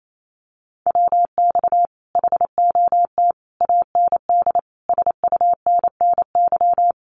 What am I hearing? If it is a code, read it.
Morse code, heard as a keyed tone, about 28 wpm, 700 Hz. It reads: WX HOT AND SUNNY
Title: Common QSO Phrases Encore 28wpm